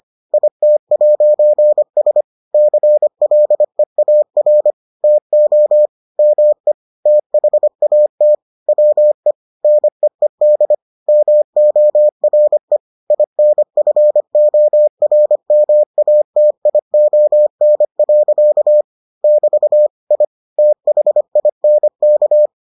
Morse code, heard as a keyed tone, about 25 wpm, 600 Hz. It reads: IT'S CLEAR TO ME THAT WE NEED MORE INFORMATION. <BT> I THINK